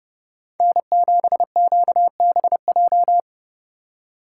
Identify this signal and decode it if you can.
Morse code, heard as a keyed tone, about 30 wpm, 700 Hz. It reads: N7QBJ